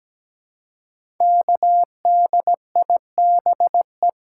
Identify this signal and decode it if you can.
Morse code, heard as a keyed tone, about 17 wpm, 700 Hz. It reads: KDIBE
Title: Code Groups - 5 Letters - 17wpm